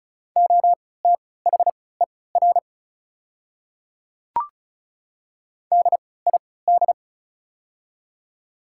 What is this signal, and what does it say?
Morse code, heard as a keyed tone, about 34 wpm, spaced as for 20 wpm, 700 Hz. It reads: OTHER DID
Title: 3x Character Spacing - Top 200 Words - 35wpm